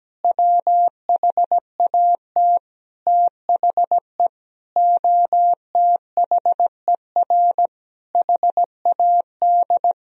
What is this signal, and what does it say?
Morse code, heard as a keyed tone, about 17 wpm, 700 Hz. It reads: WHAT THE OTHER HAD
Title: Sentences from Top 1000 Words - Review Part 1 - 17wpm